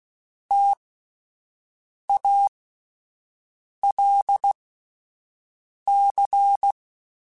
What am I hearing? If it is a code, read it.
Morse code, heard as a keyed tone, about 16 wpm, spaced as for 5 wpm, 800 Hz. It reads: TALC
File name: Alphabetical / Alphabetical mp3s / Ts @5